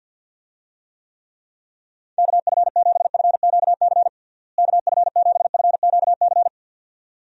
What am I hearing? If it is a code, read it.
Morse code, heard as a keyed tone, about 48 wpm, 700 Hz. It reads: KF7FQC KF7FQC
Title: USA Call Signs - 6 Characters - Repeated 2x - 50wpm